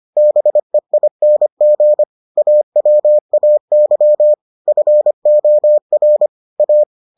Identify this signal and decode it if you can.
Morse code, heard as a keyed tone, about 25 wpm, 600 Hz. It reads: BEING AWAY FOR A